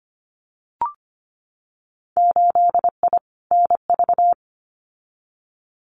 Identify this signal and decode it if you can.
Morse code, heard as a keyed tone, about 25 wpm, 700 Hz. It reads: 8I NV
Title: Winter Field Day Exchanges Encore 25wpm